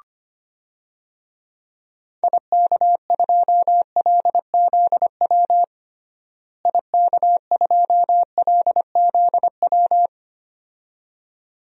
Morse code, heard as a keyed tone, about 25 wpm, 700 Hz. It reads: IK2LZW IK2LZW
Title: DXCC-100 Call Signs - 6 Characters - Repeated 2x - 25wpm